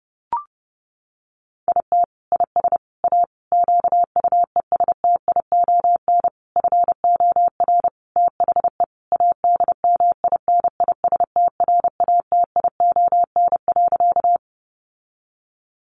Morse code, heard as a keyed tone, about 30 wpm, 700 Hz. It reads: IT IS A QUESTION FOR THE ADMINISTRATION.
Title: Sentences from Top 1300 Words 30wpm